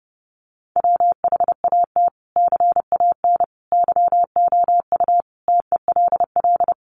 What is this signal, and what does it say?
Morse code, heard as a keyed tone, about 30 wpm, 700 Hz. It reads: WHAT CAN YOU TELL
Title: Sentences from Top 1500 Words 30wpm